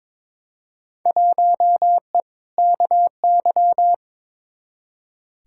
Morse code, heard as a keyed tone, about 22 wpm, 700 Hz. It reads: 1E KY
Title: Field Day Exchanges - 22wpm